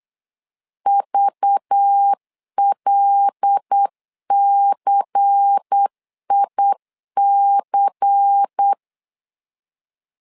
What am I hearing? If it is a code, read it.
Morse code, heard as a keyed tone, about 8 wpm, 800 Hz. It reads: VLCIC
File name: LEZIONI / 5 LBVC / 5 LBVC 8wpm 2935